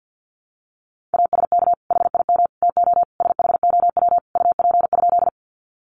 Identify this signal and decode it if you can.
Morse code, heard as a keyed tone, about 48 wpm, 700 Hz. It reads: ASK HIM TO SHOW UP?
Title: Sentences from Top 200 Words 50wpm